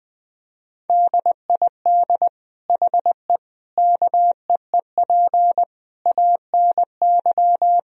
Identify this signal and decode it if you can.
Morse code, heard as a keyed tone, about 20 wpm, 700 Hz. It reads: DID HE KEEP ANY
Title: Sentences from Top 1000 Words - Review Part 3 - 20wpm